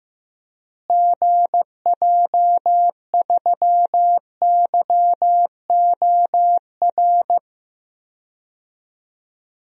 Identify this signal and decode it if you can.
Morse code, heard as a keyed tone, about 15 wpm, 700 Hz. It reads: GJ3YOR
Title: DXCC-100 Call Signs - 6 Characters - Repeated 3x - 15wpm